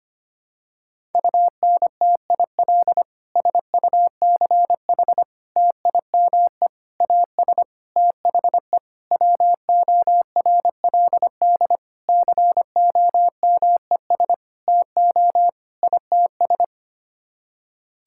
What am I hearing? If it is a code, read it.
Morse code, heard as a keyed tone, about 25 wpm, 700 Hz. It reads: UNTIL SUCH TIME AS THE WORLD COMES TO ITS
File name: Sets of 9 Words - 25wpm